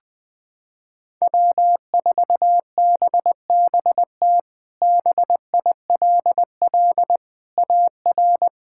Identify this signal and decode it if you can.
Morse code, heard as a keyed tone, about 21 wpm, 700 Hz. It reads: W4BBT BILL AR
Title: K1USN SST QSO Practice - 20wpm